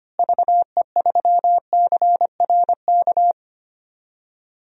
Morse code, heard as a keyed tone, about 25 wpm, 700 Hz. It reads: VE3CRK
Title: DXCC-100 Call Signs - 6 Characters - Repeated 3x - 25wpm